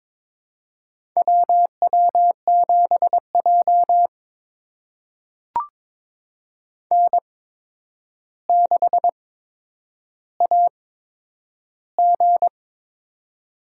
Morse code, heard as a keyed tone, about 23 wpm, spaced as for 10 wpm, 700 Hz. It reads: WW7J N6AG
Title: USA Call Signs - 4 Characters - 8x Character Spacing - 22wpm